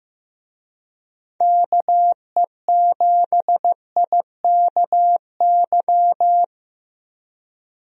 Morse code, heard as a keyed tone, about 15 wpm, 700 Hz. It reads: KE7IKY
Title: USA Call Signs - 6 Characters - Repeated 2x - 15wpm